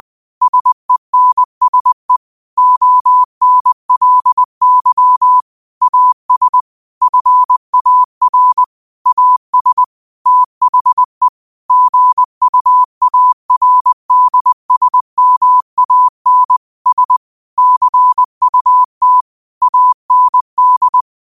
Morse code, heard as a keyed tone, about 20 wpm, 1000 Hz. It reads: SENSE ONLY AS FAR AS THE GUARDSMAN S CUT AND